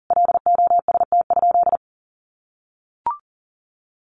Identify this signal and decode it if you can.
Morse code, heard as a keyed tone, about 40 wpm, 700 Hz. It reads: LOST?
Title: Sentences from Top 1400 Words 40wpm